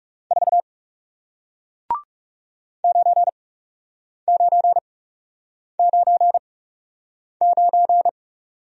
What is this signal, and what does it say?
Morse code, heard as a keyed tone, about 38 wpm, spaced as for 8 wpm, 700 Hz. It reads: 4 9999